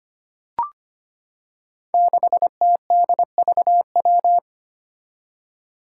Morse code, heard as a keyed tone, about 25 wpm, 700 Hz. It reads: 6TDVW
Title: Five Challenging Letter-Number Combinations 25wpm